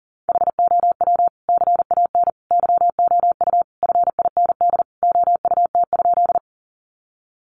Morse code, heard as a keyed tone, about 40 wpm, 700 Hz. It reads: HOW CAN YOU FIND OUT?
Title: Sentences from Top 200 Words 40wpm